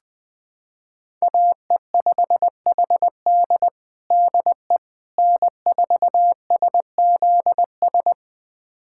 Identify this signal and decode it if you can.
Morse code, heard as a keyed tone, about 20 wpm, 700 Hz. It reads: AE5HD DE N4SZS
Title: USA Callsign DE Callsign - 5 Characters - 20wpm